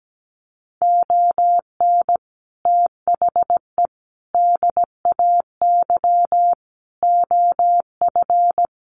Morse code, heard as a keyed tone, about 17 wpm, 700 Hz. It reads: ON THE DAY OF
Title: Sets of 5 Words Encore 17wpm